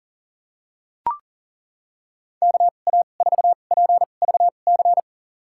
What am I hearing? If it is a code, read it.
Morse code, heard as a keyed tone, about 40 wpm, spaced as for 29 wpm, 700 Hz. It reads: KA4PVC
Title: USA Call Signs - 6 Characters - 2x Character Spacing - 40wpm